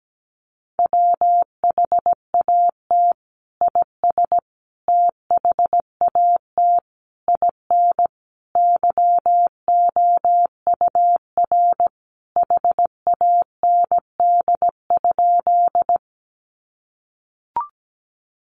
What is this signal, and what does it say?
Morse code, heard as a keyed tone, about 17 wpm, 700 Hz. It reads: WHAT IS THAT IN YOUR HAND?
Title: Sentences from Top 200 Words 17wpm